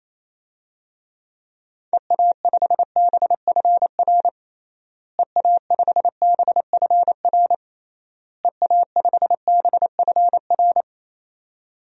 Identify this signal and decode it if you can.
Morse code, heard as a keyed tone, about 28 wpm, 700 Hz. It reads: EA5BFR EA5BFR EA5BFR
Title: DXCC-100 Call Signs - 6 Characters - Repeated 3x - 28wpm